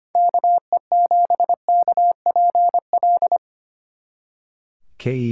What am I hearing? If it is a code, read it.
Morse code, heard as a keyed tone, about 29 wpm, 700 Hz. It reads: KE7KPL
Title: Speed Racing - USA Call Signs 25wpm